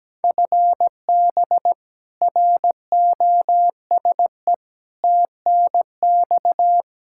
Morse code, heard as a keyed tone, about 17 wpm, 700 Hz. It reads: FB ROSE TNX
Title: Practice QSO - Exchange 3 - More Info - 17wpm